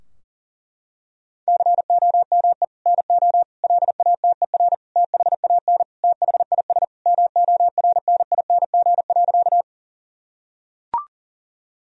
Morse code, heard as a keyed tone, about 40 wpm, 700 Hz. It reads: COME NO LATER THAN THIS MORNING.
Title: Sentences from Top 1000 Words - Review Part 4 - 40wpm